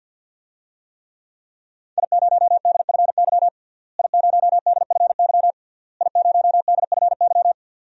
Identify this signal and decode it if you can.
Morse code, heard as a keyed tone, about 51 wpm, 700 Hz. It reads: I0DFY I0DFY I0DFY
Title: DXCC-100 Call Signs - 5 Characters - Repeated 3x - 50wpm